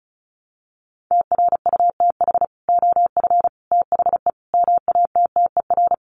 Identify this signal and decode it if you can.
Morse code, heard as a keyed tone, about 34 wpm, 700 Hz. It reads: TRUTH OF THE MATTER
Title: Sets of 5 Words Encore2 35wpm